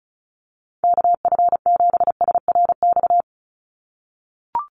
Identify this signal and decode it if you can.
Morse code, heard as a keyed tone, about 36 wpm, 700 Hz. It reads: KF7SRX
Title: USA Call Signs 35wpm